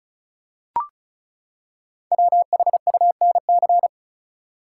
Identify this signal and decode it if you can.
Morse code, heard as a keyed tone, about 36 wpm, 700 Hz. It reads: WHUNC
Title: Code Groups - 5 Letters - 35wpm